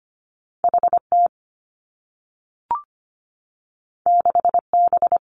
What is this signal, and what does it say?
Morse code, heard as a keyed tone, about 25 wpm, 700 Hz. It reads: H T 6 B
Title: Two Challenging Letter-Number Combinations 25wpm